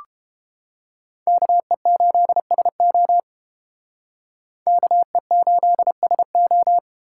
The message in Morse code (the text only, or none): KE8SO KE8SO